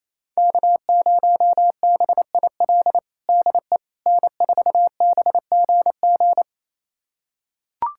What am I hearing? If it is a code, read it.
Morse code, heard as a keyed tone, about 27 wpm, 700 Hz. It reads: K0BIL DE N4BGG
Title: USA Callsign DE Callsign - 5 Characters - 28wpm